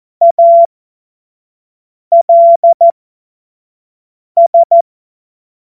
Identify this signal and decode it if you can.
Morse code, heard as a keyed tone, about 14 wpm, spaced as for 5 wpm, 680 Hz. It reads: ALS
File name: Morsekurs / Witze / Witz 24.Oktober 2023 25 F70